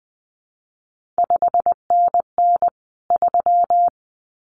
Morse code, heard as a keyed tone, about 19 wpm, 700 Hz. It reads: 5NN 3
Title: CQ WW DX Contest Exchange Encore - 20wpm